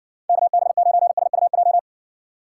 Morse code, heard as a keyed tone, about 60 wpm, 700 Hz. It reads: KB9SUJ